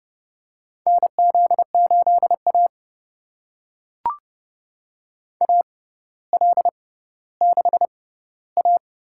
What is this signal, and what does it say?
Morse code, heard as a keyed tone, about 30 wpm, spaced as for 15 wpm, 700 Hz. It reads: NZ8A AL6A
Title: USA Call Signs - 4 Characters - 6x Character Spacing - 30wpm